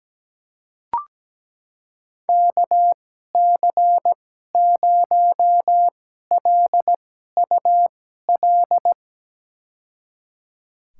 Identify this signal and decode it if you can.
Morse code, heard as a keyed tone, about 17 wpm, spaced as for 12 wpm, 700 Hz. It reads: KC0LUL
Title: USA Call Signs - 2x Character Spacing - 17wpm